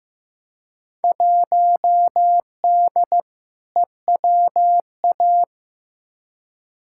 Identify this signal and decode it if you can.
Morse code, heard as a keyed tone, about 15 wpm, 700 Hz. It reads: 1D EWA
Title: Field Day Exchanges - 15wpm